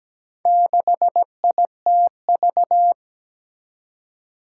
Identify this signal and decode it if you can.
Morse code, heard as a keyed tone, about 17 wpm, 700 Hz. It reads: 6ITV
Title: Four Challenging Letter-Number Combinations 17wpm